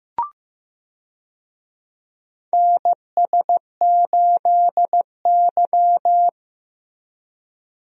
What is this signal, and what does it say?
Morse code, heard as a keyed tone, about 15 wpm, 700 Hz. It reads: NS8Y
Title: USA Call Signs - 4 Characters - Repeated 2x - 15wpm